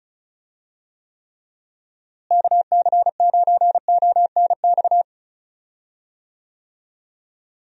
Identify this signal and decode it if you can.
Morse code, heard as a keyed tone, about 36 wpm, 700 Hz. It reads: KC9ONX
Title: USA Call Signs - 6 Characters - Repeated 3x - 35wpm